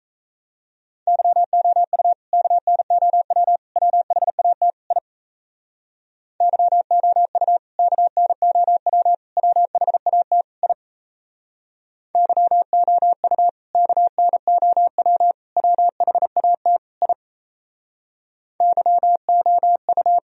Morse code, heard as a keyed tone, about 36 wpm, 700 Hz. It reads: YOU KNOW WHAT I YOU KNOW WHAT I YOU KNOW WHAT I YOU